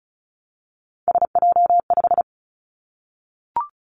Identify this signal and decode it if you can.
Morse code, heard as a keyed tone, about 36 wpm, 700 Hz. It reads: SJ5
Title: Three Challenging Letter-Number Combinations 35wpm